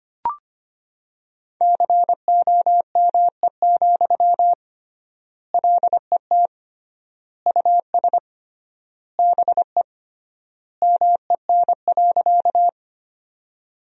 Morse code, heard as a keyed tone, about 25 wpm, 700 Hz. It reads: COME, LET US BE MEN.
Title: 3x Word-Spacing - Sentences from Top 300 Words - 25wpm